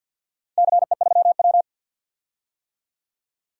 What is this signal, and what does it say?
Morse code, heard as a keyed tone, about 52 wpm, 700 Hz. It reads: CE3W
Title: DXCC-100 Call Signs - 4 Characters - 50wpm